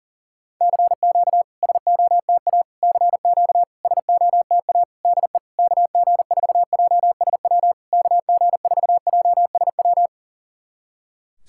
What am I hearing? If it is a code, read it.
Morse code, heard as a keyed tone, about 44 wpm, 700 Hz. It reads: CQ SOTA CQ SOTA DE KG4JSW KG4JSW
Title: SOTA QSO Practice 40wpm